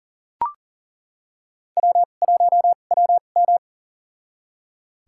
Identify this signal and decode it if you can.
Morse code, heard as a keyed tone, about 40 wpm, spaced as for 29 wpm, 700 Hz. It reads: W1WM
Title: USA Call Signs - 4 Characters - 2x Character Spacing - 40wpm